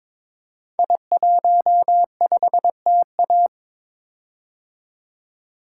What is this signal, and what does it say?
Morse code, heard as a keyed tone, about 23 wpm, 700 Hz. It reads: I15TA
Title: Code Groups - 5 Letter:Numbers - 22wpm